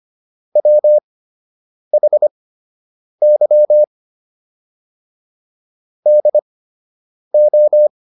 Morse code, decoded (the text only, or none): WHY DO